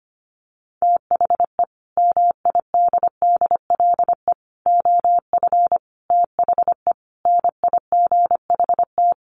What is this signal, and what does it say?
Morse code, heard as a keyed tone, about 25 wpm, 700 Hz. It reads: THE MIDDLE OF THE NIGHT
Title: Sets of 5 Words 25wpm